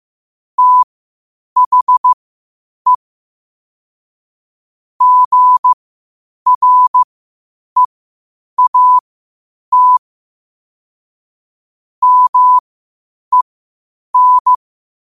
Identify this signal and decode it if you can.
Morse code, heard as a keyed tone, about 15 wpm, spaced as for 8 wpm, 1000 Hz. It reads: THE GREAT MEN
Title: 8 WPM morse code quotes for Fri, 01 Mar 2024